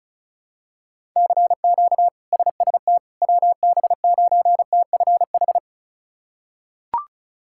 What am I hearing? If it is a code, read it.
Morse code, heard as a keyed tone, about 36 wpm, 700 Hz. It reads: CQ SST WB9TFH
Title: K1USN SST QSO Practice - 35wpm